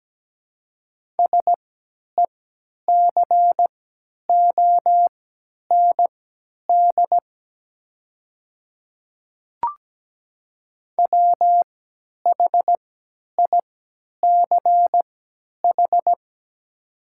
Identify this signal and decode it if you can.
Morse code, heard as a keyed tone, about 17 wpm, spaced as for 10 wpm, 700 Hz. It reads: SECOND WHICH
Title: 3x Character Spacing - Top 300 Words - 17wpm